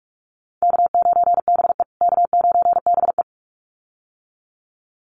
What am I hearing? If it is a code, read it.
Morse code, heard as a keyed tone, about 44 wpm, 700 Hz. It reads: K9BE K9BE
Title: USA Call Signs Repeated 2x 45wpm